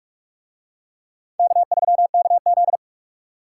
Morse code, heard as a keyed tone, about 43 wpm, 700 Hz. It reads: K3KZ